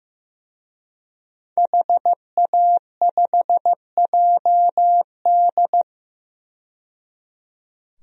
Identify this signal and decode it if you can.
Morse code, heard as a keyed tone, about 15 wpm, 700 Hz. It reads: HA5JD
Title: DXCC-100 Call Signs - 5 Characters - 15wpm